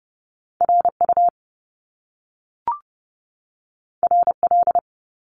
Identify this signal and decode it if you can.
Morse code, heard as a keyed tone, about 30 wpm, 700 Hz. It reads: RU RL
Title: Top Two Letter Combinations in Words 30wpm